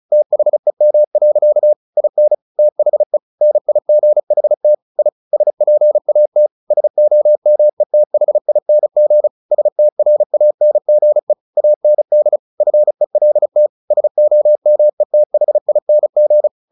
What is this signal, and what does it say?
Morse code, heard as a keyed tone, about 34 wpm, 600 Hz. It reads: THEM. IN THE NIGHT I SPAT SOMETHING STRANGE AND FELT SOMETHING
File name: the old man and the sea 2 35wpm